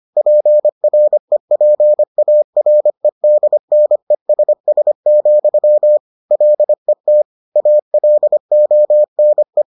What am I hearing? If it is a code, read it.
Morse code, heard as a keyed tone, about 25 wpm, 600 Hz. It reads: PREPAREDNESS, LET ALONE